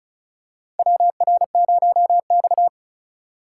Morse code, heard as a keyed tone, about 36 wpm, 690 Hz. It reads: WR0X
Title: USA Call Signs - 4 Characters - 3x Character Spacing - 35wpm